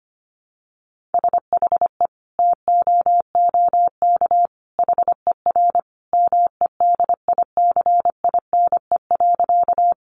Morse code, heard as a keyed tone, about 25 wpm, 700 Hz. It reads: SHE TOOK HER MEDICINE.
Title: Sentences from Top 900 Words 25wpm